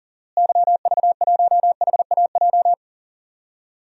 Morse code, heard as a keyed tone, about 40 wpm, 700 Hz. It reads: YV1HAJ
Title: DXCC-100 Call Signs - 6 Characters - Repeated 2x - 40wpm